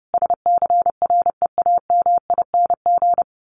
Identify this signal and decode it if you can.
Morse code, heard as a keyed tone, about 30 wpm, 700 Hz. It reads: SCREAMING
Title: A Sound Of Thunder 30wpm